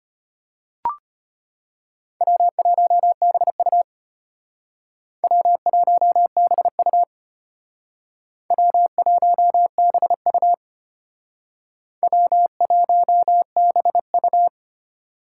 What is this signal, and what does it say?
Morse code, heard as a keyed tone, about 33 wpm, 700 Hz. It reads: W1BU W1BU W1BU W1BU